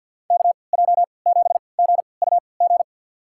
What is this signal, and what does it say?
Morse code, heard as a keyed tone, about 53 wpm, spaced as for 29 wpm, 700 Hz. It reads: KJ7GUG
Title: USA Call Signs - 6 Characters - 3x Character Spacing - 50wpm